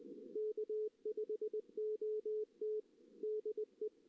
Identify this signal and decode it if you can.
Morse code, heard as a keyed tone, about 19 wpm, 440 Hz. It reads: K5OT DE
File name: 2016 JAN CW NAQP / 2016 Jan naqp k5ot - 0-2